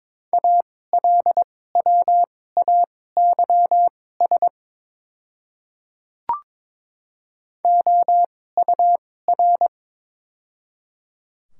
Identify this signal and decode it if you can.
Morse code, heard as a keyed tone, about 22 wpm, spaced as for 16 wpm, 700 Hz. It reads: ALWAYS OUR
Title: 2x Character Spacing - Top 300 Words - 22wpm